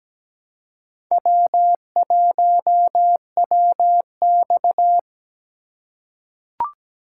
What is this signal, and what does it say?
Morse code, heard as a keyed tone, about 17 wpm, 700 Hz. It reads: W1WX
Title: USA Call Signs - 4 Characters - 3x Character Spacing - 17wpm